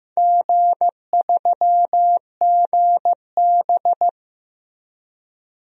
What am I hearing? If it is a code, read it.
Morse code, heard as a keyed tone, about 15 wpm, 700 Hz. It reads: G3GB
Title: DXCC-100 Call Signs - 4 Characters - 15wpm